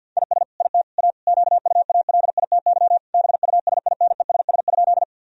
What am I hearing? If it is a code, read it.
Morse code, heard as a keyed tone, about 50 wpm, 700 Hz. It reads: IS IT A QUALITY BUSINESS?
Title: Sentences from Top 1000 Words - Review Part 4 - 50wpm